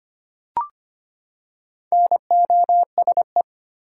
Morse code, heard as a keyed tone, about 25 wpm, 700 Hz. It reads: NOSE
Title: Four Letter Words - 25wpm